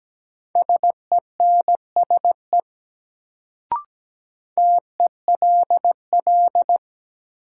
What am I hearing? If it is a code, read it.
Morse code, heard as a keyed tone, about 17 wpm, 700 Hz. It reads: SENSE TELL
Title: Sets of 2 Words - New Word Drill - 17wpm